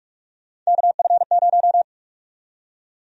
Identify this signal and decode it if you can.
Morse code, heard as a keyed tone, about 47 wpm, 700 Hz. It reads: KF0
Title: USA Call Sign Prefixes - Repeated 2x - 45wpm